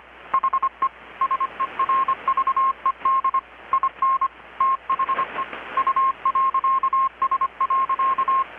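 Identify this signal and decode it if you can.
Morse code, heard as a keyed tone, about 25 wpm, 1070 Hz. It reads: HE SERVED IN THE U.S.
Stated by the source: Morse from KPH's Night of Nights XX maritime radiotelegraphy event, received in Chiapas, Mexico about 3,800 km away on a Kenwood R-600 with a grounded 10 m long-wire antenna.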